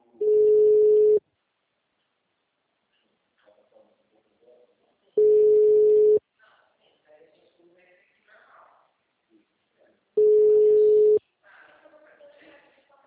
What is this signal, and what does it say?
Ringback tone (Europe, 425 Hz), with speech.